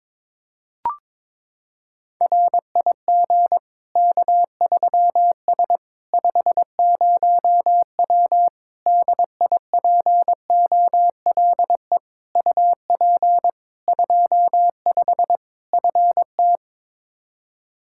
Morse code, heard as a keyed tone, about 22 wpm, 700 Hz. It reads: RIG K3S 50W DIPOLE UP 25 FT
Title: Practice QSO - Exchange 2 - Info - 22wpm